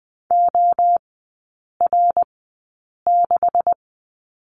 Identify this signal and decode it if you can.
Morse code, heard as a keyed tone, about 20 wpm, spaced as for 8 wpm, 700 Hz. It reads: OR6
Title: 212 Lesson 34e - Number 8 - Three Characters 20wpm